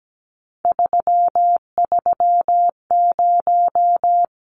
Morse code, heard as a keyed tone, about 17 wpm, 700 Hz. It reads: 330
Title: Numbers - 3-Digit 17wpm